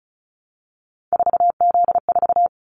Morse code, heard as a keyed tone, about 34 wpm, 700 Hz. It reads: 4Z4
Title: Callsign Prefixes for Top 50 DXCC - 35wpm